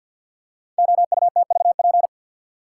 Morse code, heard as a keyed tone, about 47 wpm, 700 Hz. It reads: OUTVP
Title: Code Groups - 5 Letter:Numbers - 50wpm